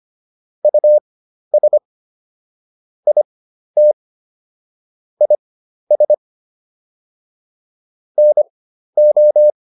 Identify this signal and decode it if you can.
Morse code, heard as a keyed tone, about 25 wpm, spaced as for 12 wpm, 600 Hz. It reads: US IT IS NO